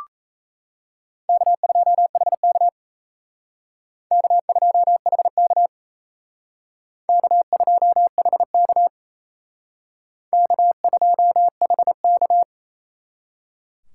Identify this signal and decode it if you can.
Morse code, heard as a keyed tone, about 34 wpm, 700 Hz. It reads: K2HK K2HK K2HK K2HK
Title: Speed Racing - USA 1x2 Call Signs 28wpm